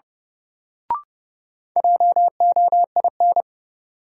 Morse code, heard as a keyed tone, about 32 wpm, 700 Hz. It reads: JOIN
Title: Sentences from Top 1000 Words - Review Part 6 - New Word Drill - 30wpm